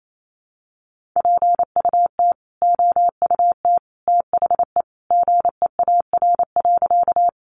Morse code, heard as a keyed tone, about 28 wpm, 700 Hz. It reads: PUT OUT THE GEAR.
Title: Sentences from Top 1400 Words 28wpm